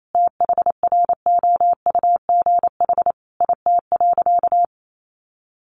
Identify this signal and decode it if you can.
Morse code, heard as a keyed tone, about 28 wpm, 700 Hz. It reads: THROUGH IT.
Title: Sentences from Top 200 Words 28wpm